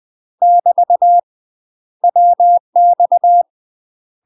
Morse code, heard as a keyed tone, about 20 wpm, 700 Hz. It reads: <BT> WX
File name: UK MASTER QSOs TRAINING / 20WPM / UK MASTER QSOs 0044